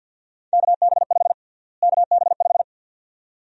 Morse code, heard as a keyed tone, about 47 wpm, 700 Hz. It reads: KB5 KB5
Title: USA Call Sign Prefixes - Repeated 2x - 50wpm